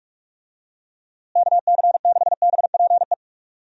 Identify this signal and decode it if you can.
Morse code, heard as a keyed tone, about 44 wpm, 700 Hz. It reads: KC6BPE
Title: USA Call Signs - 6 Characters - Repeated 2x - 45wpm